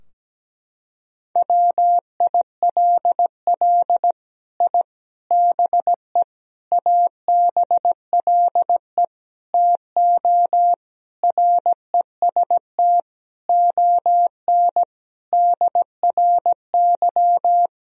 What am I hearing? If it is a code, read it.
Morse code, heard as a keyed tone, about 17 wpm, 700 Hz. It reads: WILL I BE ABLE TO REST ON DRY
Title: Sentences from Top 1000 Words - Review Part 3 - 17wpm